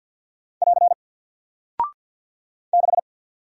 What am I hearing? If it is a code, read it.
Morse code, heard as a keyed tone, about 48 wpm, 700 Hz. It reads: <AR> 6
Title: Single Character (Letter:Number:Punctuation:Prosign) Alt 50wpm